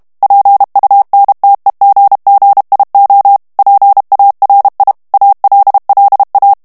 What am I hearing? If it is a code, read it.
Morse code, heard as a keyed tone, about 32 wpm, 800 Hz. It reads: PUNTEGGIO PARI ALLA
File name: ITALIEN texte clair 2 vitesse 30 mots minute